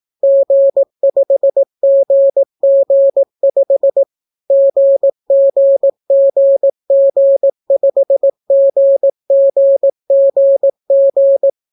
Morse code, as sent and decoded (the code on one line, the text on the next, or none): --. ..... --. --. ..... / --. --. --. --. ..... --. --. --. --.
G5GG5 GGGG5GGGG